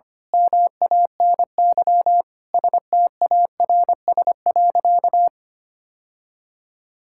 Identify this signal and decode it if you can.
Morse code, heard as a keyed tone, about 25 wpm, 700 Hz. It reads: MANY STARS.
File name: practice / 25wpm / Sentences with 2-Word Phrases - 25wpm